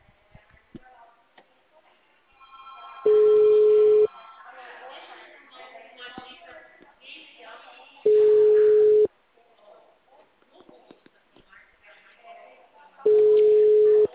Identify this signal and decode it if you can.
Ringback tone (Europe, 425 Hz), with speech.